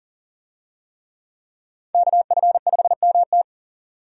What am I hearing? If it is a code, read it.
Morse code, heard as a keyed tone, about 38 wpm, 700 Hz. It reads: KF5MT
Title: USA Call Signs - 5 Characters - 40wpm